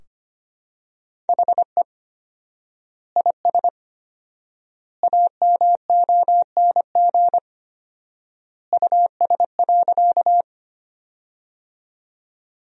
Morse code, heard as a keyed tone, about 25 wpm, 700 Hz. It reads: HE IS AMONG US.
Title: 4x Word-Spacing - Sentences from Top 500 Words - 25wpm